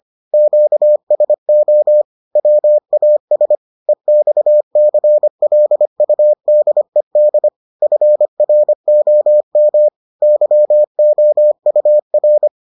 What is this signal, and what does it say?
Morse code, heard as a keyed tone, about 25 wpm, 600 Hz. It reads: QSO WAS EXCLUDED FROM YOUR